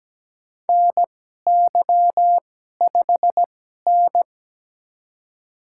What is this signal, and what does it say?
Morse code, heard as a keyed tone, about 17 wpm, spaced as for 12 wpm, 700 Hz. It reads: NY5N
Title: USA Call Signs - 4 Characters - 2x Character Spacing - 17wpm